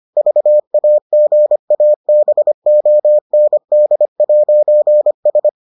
Morse code, heard as a keyed tone, about 25 wpm, 600 Hz. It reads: VAGABOND'S